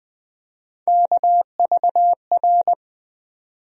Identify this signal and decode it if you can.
Morse code, heard as a keyed tone, about 20 wpm, 700 Hz. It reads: KVR
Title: USA Call Sign Postfixes - 3 Letters 20wpm